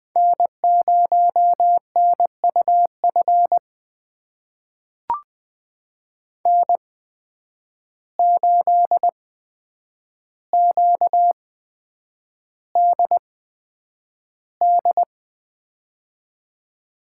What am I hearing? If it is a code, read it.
Morse code, heard as a keyed tone, about 21 wpm, spaced as for 9 wpm, 700 Hz. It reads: N0NUF N8QDD
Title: USA Call Signs - 5 Characters - 8x Character Spacing - 20wpm